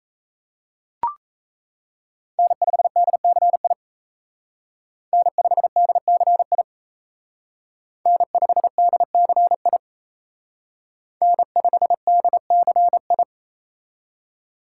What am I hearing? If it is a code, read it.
Morse code, heard as a keyed tone, about 36 wpm, 700 Hz. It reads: N5DCI N5DCI N5DCI N5DCI